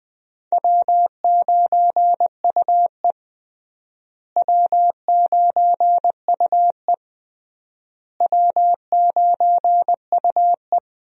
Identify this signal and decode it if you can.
Morse code, heard as a keyed tone, about 20 wpm, 700 Hz. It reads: W9UE W9UE W9UE
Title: USA Call Signs - 4 Characters - Repeated 3x - 20wpm